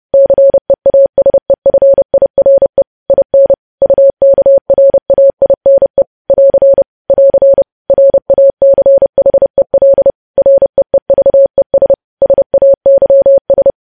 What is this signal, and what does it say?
Morse code, heard as a keyed tone, about 30 wpm, 550 Hz. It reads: CEASEFIRE IN UKRAINE <AR> <AR> RACHEL REEVES SAYS
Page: news headlines in morse code at 30 wpm